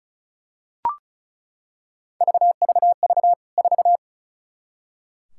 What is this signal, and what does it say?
Morse code, heard as a keyed tone, about 34 wpm, 700 Hz. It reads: VVV 4
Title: K1USN SST QSO Practice 35wpm